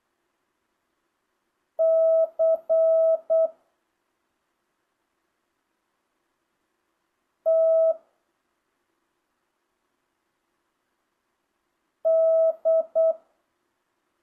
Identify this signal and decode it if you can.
Morse code, heard as a keyed tone, about 8 wpm, 650 Hz. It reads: C T D
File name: jh8cbh / 2025cw